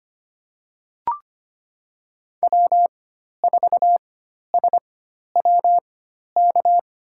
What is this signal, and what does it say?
Morse code, heard as a keyed tone, about 25 wpm, spaced as for 12 wpm, 700 Hz. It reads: W4SWK
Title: USA Call Signs - 5 Characters - 4x Character Spacing - 25wpm